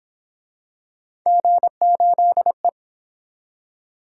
Morse code, heard as a keyed tone, about 29 wpm, 700 Hz. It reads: G8E